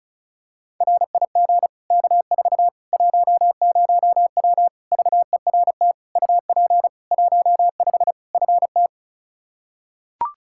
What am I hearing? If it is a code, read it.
Morse code, heard as a keyed tone, about 36 wpm, 700 Hz. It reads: RIG K4 10W VERT UP 15 FT
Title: Practice QSO - Exchange 2 - Info - 35wpm